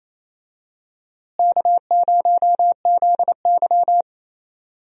Morse code, heard as a keyed tone, about 27 wpm, 700 Hz. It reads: K0ZY
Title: USA Call Signs - 4 Characters - Repeated 2x - 28wpm